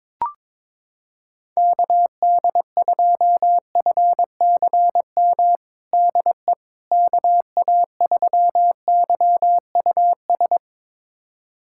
Morse code, heard as a keyed tone, about 22 wpm, 700 Hz. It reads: KD2FCM DE KA3YUS
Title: USA Callsign DE Callsign - 6 Characters - 22wpm